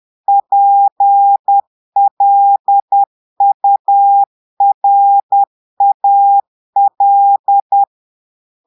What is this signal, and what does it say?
Morse code, heard as a keyed tone, about 10 wpm, 800 Hz. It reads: PLURAL
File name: Portugais - texte en clair 1 - vitesse - 10 mots minute